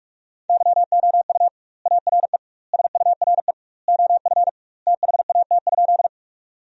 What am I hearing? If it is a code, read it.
Morse code, heard as a keyed tone, about 46 wpm, 700 Hz. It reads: YOU ARE SURE OF THAT?
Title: 2x Word-Spacing - Sentences from Top 300 Words - 45wpm